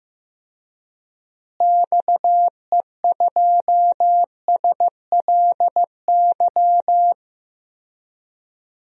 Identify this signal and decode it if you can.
Morse code, heard as a keyed tone, about 15 wpm, 700 Hz. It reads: XE2SLY
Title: DXCC-100 Call Signs - 6 Characters - Repeated 3x - 15wpm